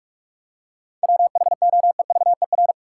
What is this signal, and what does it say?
Morse code, heard as a keyed tone, about 47 wpm, 700 Hz. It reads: WHOEVER
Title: Seven Letter Words - 45wpm